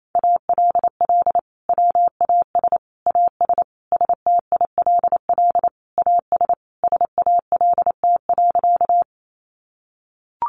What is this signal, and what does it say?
Morse code, heard as a keyed tone, about 28 wpm, 700 Hz. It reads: ALL WAS AS STILL AS SALT.
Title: Sentences from Top 800 Words 28wpm